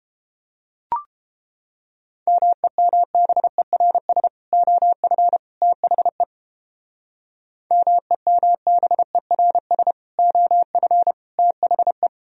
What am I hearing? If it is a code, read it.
Morse code, heard as a keyed tone, about 29 wpm, 700 Hz. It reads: MEMBERS OF THE MEMBERS OF THE